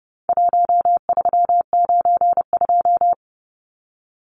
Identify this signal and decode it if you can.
Morse code, heard as a keyed tone, about 30 wpm, 700 Hz. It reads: 1392
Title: Numbers - Contest Style - 4-Digit 30wpm